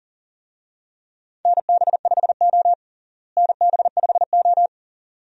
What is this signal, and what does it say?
Morse code, heard as a keyed tone, about 40 wpm, 700 Hz. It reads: NB5O NB5O
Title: USA Call Signs - 4 Characters - Repeated 2x - 40wpm